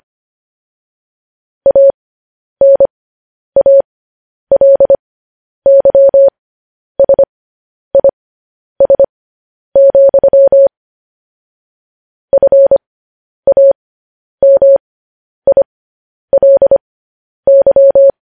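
Morse code, heard as a keyed tone, about 25 wpm, spaced as for 10 wpm, 550 Hz. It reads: ANALYSIS, FAMILY
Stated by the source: Morse code news headlines practice transmission.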